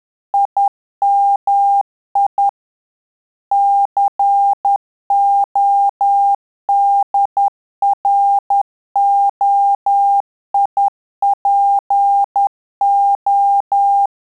Morse code, heard as a keyed tone, about 10 wpm, 800 Hz. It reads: IMI CODROIPO
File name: Telegraphy-group / QSO CW 10 PPM